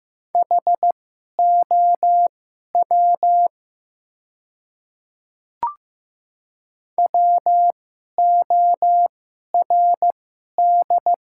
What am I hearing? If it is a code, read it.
Morse code, heard as a keyed tone, about 15 wpm, spaced as for 11 wpm, 700 Hz. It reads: HOW WORD
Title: 2x Character Spacing - Top 100 Words - 15wpm